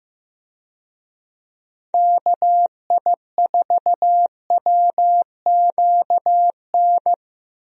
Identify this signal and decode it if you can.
Morse code, heard as a keyed tone, about 15 wpm, 700 Hz. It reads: KI4WQN
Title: USA Call Signs - 6 Characters - Repeated 2x - 15wpm